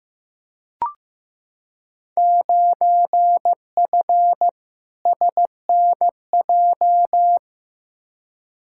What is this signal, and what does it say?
Morse code, heard as a keyed tone, about 15 wpm, 700 Hz. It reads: 9F SNJ
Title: Field Day Exchanges Encore - 15wpm